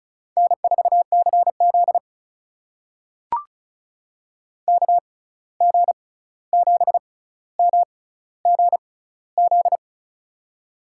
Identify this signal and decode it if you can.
Morse code, heard as a keyed tone, about 36 wpm, spaced as for 16 wpm, 700 Hz. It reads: N4CZ KG7MGZ
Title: USA Call Signs - 6x Character Spacing - 35wpm